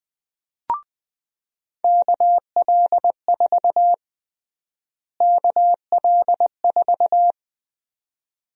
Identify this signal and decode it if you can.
Morse code, heard as a keyed tone, about 20 wpm, 700 Hz. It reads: KL4 KL4
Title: USA Call Sign Prefixes - Repeated 2x - 20wpm